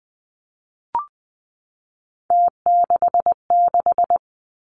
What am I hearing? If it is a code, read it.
Morse code, heard as a keyed tone, about 20 wpm, 700 Hz. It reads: T66
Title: Three Challenging Letter-Number Combinations 20wpm